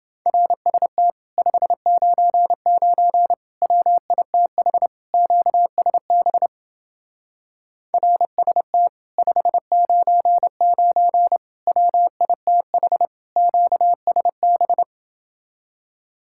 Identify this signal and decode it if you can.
Morse code, heard as a keyed tone, about 29 wpm, 700 Hz. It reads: RST 599 WITH QSB RST 599 WITH QSB